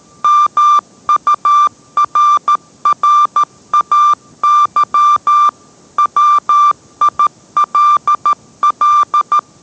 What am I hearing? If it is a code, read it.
Morse code, heard as a keyed tone, about 13 wpm, 1180 Hz. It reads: MURRAY WILL